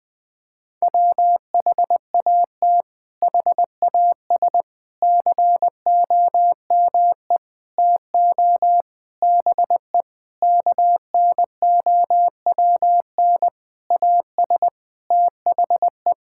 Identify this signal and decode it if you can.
Morse code, heard as a keyed tone, about 21 wpm, 700 Hz. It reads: WHAT HAS COME TO BE KNOWN AS THE
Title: Sentences with 8-Word Phrases - 20wpm